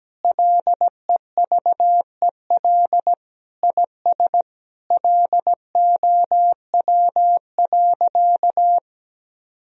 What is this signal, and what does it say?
Morse code, heard as a keyed tone, about 17 wpm, 700 Hz. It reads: LEVEL IS LOW.
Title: Sentences from Top 1000 Words - Review Part 4 - 17wpm